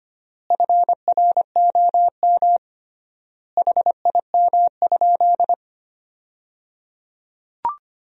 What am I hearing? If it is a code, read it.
Morse code, heard as a keyed tone, about 25 wpm, 700 Hz. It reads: FROM HIM?
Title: 3x Word-Spacing - Sentences from Top 300 Words - 25wpm